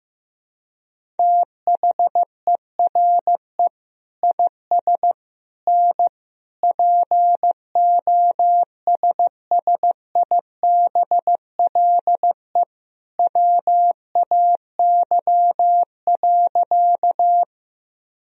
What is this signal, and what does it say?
Morse code, heard as a keyed tone, about 15 wpm, 700 Hz. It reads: THERE IS N POSSIBLE WAY.
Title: Sentences from Top 500 Words and Abbreviations 15wpm